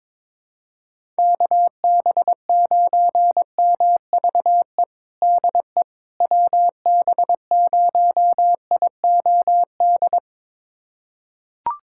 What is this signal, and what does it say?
Morse code, heard as a keyed tone, about 22 wpm, 700 Hz. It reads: KB9MVE DE WB0IOD
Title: USA Callsign DE Callsign 22wpm